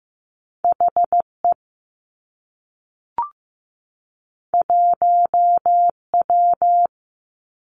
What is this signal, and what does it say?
Morse code, heard as a keyed tone, about 15 wpm, 700 Hz. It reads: H E 1 W
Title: Two Challenging Letter-Number Combinations 15wpm